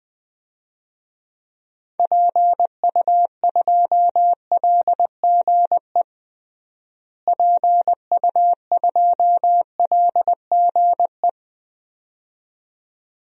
Morse code, heard as a keyed tone, about 20 wpm, 700 Hz. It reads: PU2LGE PU2LGE
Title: DXCC-100 Call Signs - 6 Characters - Repeated 2x - 20wpm